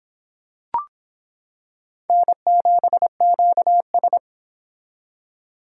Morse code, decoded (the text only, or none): N7QS